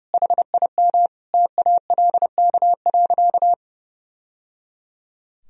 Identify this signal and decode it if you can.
Morse code, heard as a keyed tone, about 30 wpm, 700 Hz. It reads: HIM TALK.
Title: Sentences from Top 1000 Words - Review Part 4 - 30wpm